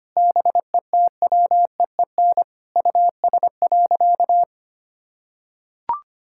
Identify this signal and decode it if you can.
Morse code, heard as a keyed tone, about 25 wpm, 700 Hz. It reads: BETWEEN US.
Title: Sentences from Top 1000 Words - Review Part 6 - 25wpm